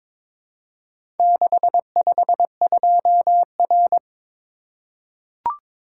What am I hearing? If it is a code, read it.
Morse code, heard as a keyed tone, about 22 wpm, 700 Hz. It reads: 652R
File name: practice / 22wpm / Four Letter-Number Combinations - 22wpm